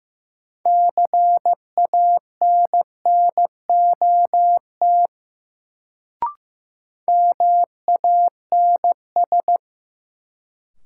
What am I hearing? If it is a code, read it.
Morse code, heard as a keyed tone, about 15 wpm, 700 Hz. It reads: CANNOT MANS
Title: Sentences from Top 1000 Words - Review Part 6 - New Word Drill - 15wpm